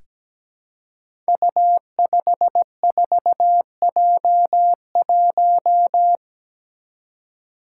Morse code, heard as a keyed tone, about 17 wpm, 700 Hz. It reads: U54J1
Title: Five Challenging Letter-Number Combinations 17wpm